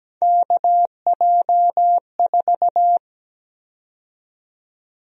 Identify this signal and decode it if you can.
Morse code, heard as a keyed tone, about 17 wpm, 700 Hz. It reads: KJ4
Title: USA Call Sign Prefixes - 17wpm